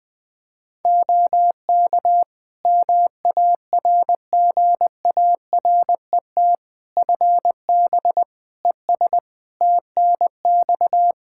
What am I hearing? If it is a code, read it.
Morse code, heard as a keyed tone, about 21 wpm, 700 Hz. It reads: OK MARGARET FB ES TNX
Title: Practice QSO - Exchange 2 - Info - 20wpm